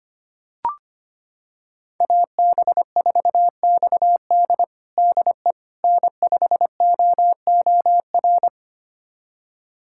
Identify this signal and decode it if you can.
Morse code, heard as a keyed tone, about 25 wpm, 700 Hz. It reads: AB4XD DE N5OOR
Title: USA Callsign DE Callsign - 5 Characters - 25wpm